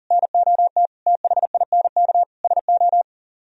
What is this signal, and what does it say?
Morse code, heard as a keyed tone, about 50 wpm, 700 Hz. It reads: NOT THINK SO
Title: Speed-Racing - Sets of 5 Words 40wpm